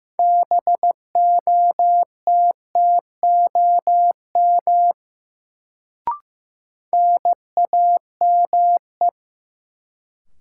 Morse code, heard as a keyed tone, about 15 wpm, 700 Hz. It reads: BOTTOM NAME
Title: Sets of 3 Words - New Word Drill - 15wpm